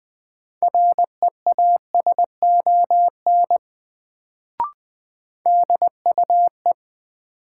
Morse code, heard as a keyed tone, about 21 wpm, 700 Hz. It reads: REASON DUE
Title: Sets of 5 Words Encore - New Word Drill - 20wpm